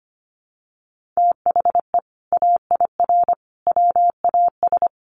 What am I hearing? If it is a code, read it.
Morse code, heard as a keyed tone, about 25 wpm, 700 Hz. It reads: THE AIR WAS
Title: Sentences from Top 1200 Words 25wpm